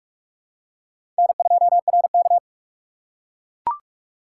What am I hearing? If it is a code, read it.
Morse code, heard as a keyed tone, about 42 wpm, 700 Hz. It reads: N2RK
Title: USA Call Signs - 4 Characters - 4x Character Spacing - 45wpm